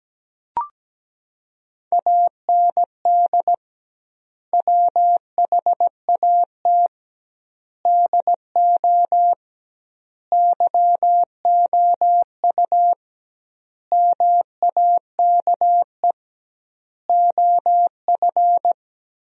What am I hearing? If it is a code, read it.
Morse code, heard as a keyed tone, about 17 wpm, 700 Hz. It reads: AND WHAT DO YOU MAKE OF
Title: 2x Word-Spacing - Sentences from Top 100 Words - 17wpm